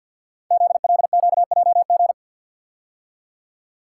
Morse code, heard as a keyed tone, about 48 wpm, 700 Hz. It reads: ZLQJG
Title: Code Groups - 5 Letters - 50wpm